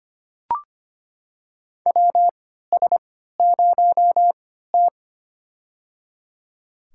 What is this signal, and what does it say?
Morse code, heard as a keyed tone, about 25 wpm, spaced as for 14 wpm, 700 Hz. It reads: WS0T
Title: USA Call Signs - 4 Characters - 3x Character Spacing - 25wpm